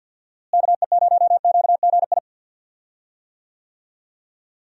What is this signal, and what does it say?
Morse code, heard as a keyed tone, about 50 wpm, 700 Hz. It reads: KE0QGI
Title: USA Call Signs - 6 Characters - Repeated 2x - 50wpm